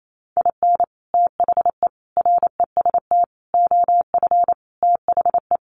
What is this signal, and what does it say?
Morse code, heard as a keyed tone, about 28 wpm, 700 Hz. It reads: IN THE REST OF THE
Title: Sets of 5 Words 28wpm